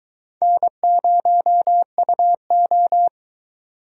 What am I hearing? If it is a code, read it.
Morse code, heard as a keyed tone, about 23 wpm, 700 Hz. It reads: N0UO